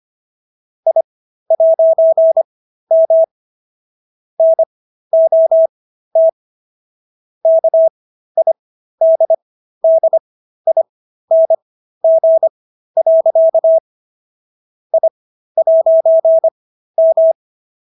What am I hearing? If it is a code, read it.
Morse code, heard as a keyed tone, about 25 wpm, spaced as for 13 wpm, 650 Hz. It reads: I'M NOT KIDDING. I'M
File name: SS106 13